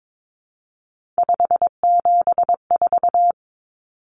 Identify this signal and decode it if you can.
Morse code, heard as a keyed tone, about 22 wpm, 700 Hz. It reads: 574
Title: Common Cut Numbers - 3-Digit 22wpm